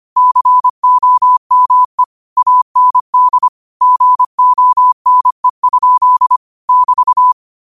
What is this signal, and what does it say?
Morse code, heard as a keyed tone, about 25 wpm, 1000 Hz. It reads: COME AND GONE? <BT>
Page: New quotes every day in morse code at 25 Words per minute.